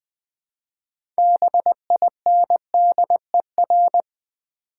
Morse code, heard as a keyed tone, about 20 wpm, 700 Hz. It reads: BINDER
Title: Easy Words with Hints - 20wpm